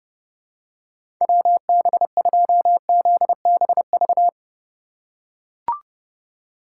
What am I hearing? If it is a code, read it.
Morse code, heard as a keyed tone, about 30 wpm, 700 Hz. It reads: WB2ZBV
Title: USA Call Signs - 6 Characters - 2x Character Spacing - 30wpm